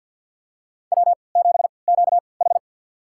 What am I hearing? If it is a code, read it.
Morse code, heard as a keyed tone, about 51 wpm, spaced as for 28 wpm, 700 Hz. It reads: W7QH
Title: USA Call Signs - 3x Character Spacing - 50wpm